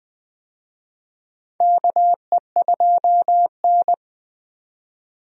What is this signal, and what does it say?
Morse code, heard as a keyed tone, about 20 wpm, 700 Hz. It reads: KE2N
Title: USA Call Signs - 4 Characters - 20wpm